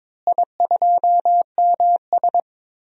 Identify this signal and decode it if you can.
Morse code, heard as a keyed tone, about 22 wpm, 700 Hz. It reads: I2MS
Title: DXCC-100 Call Signs - 4 Characters - Repeated 2x - 22wpm